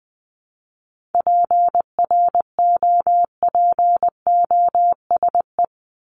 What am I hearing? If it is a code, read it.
Morse code, heard as a keyed tone, about 20 wpm, 700 Hz. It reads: PROPOSE
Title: ED Suffix - Top 100 Words with Base Words 20wpm